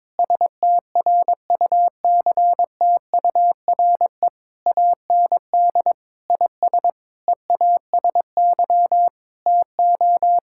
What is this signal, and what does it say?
Morse code, heard as a keyed tone, about 21 wpm, 700 Hz. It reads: STRUCTURE AND IS EASY TO
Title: Sentences from Top 1000 Words - Review Part 6 - 22wpm